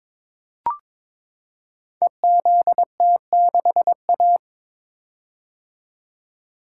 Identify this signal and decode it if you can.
Morse code, heard as a keyed tone, about 22 wpm, 700 Hz. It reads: EZT6A
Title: Code Groups - 5 Letter:Numbers - 22wpm